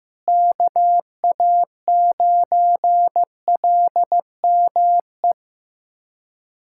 Morse code, heard as a keyed tone, about 15 wpm, 700 Hz. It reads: KA9LME
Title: USA Callsign DE Callsign 15wpm